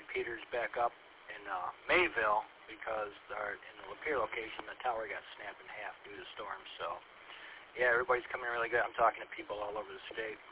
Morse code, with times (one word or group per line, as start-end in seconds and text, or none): none